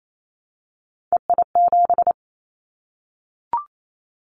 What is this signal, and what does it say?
Morse code, heard as a keyed tone, about 28 wpm, 700 Hz. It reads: EI7
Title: Callsign Prefixes for Top 50 DXCC - 28wpm